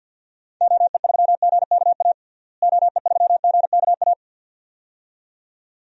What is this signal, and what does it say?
Morse code, heard as a keyed tone, about 52 wpm, 700 Hz. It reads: OE3GKA OE3GKA
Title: DXCC-100 Call Signs - 6 Characters - Repeated 2x - 50wpm